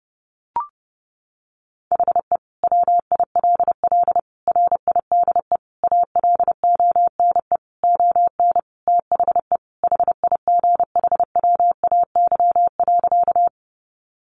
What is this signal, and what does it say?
Morse code, heard as a keyed tone, about 28 wpm, 700 Hz. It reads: HE WILL RIDE ALONE ON THE HIGHWAY.
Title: Sentences from Top 1400 Words 30wpm